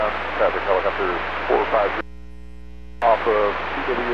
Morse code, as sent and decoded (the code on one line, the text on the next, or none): none